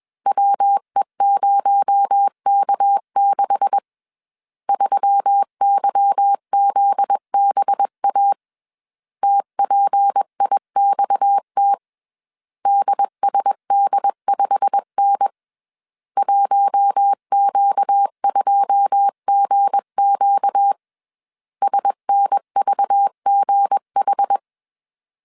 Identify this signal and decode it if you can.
Morse code, heard as a keyed tone, about 21 wpm, 800 Hz. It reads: WE0K6 3YZBA TPIXT DSD5N 1Q2GQ SNVGH
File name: CW / DET / 1 9 1500C S C 20wpm 2310